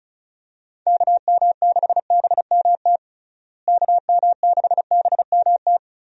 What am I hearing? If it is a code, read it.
Morse code, heard as a keyed tone, about 34 wpm, 700 Hz. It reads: KM6BMT KM6BMT
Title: USA Call Signs - 6 Characters - Repeated 3x - 35wpm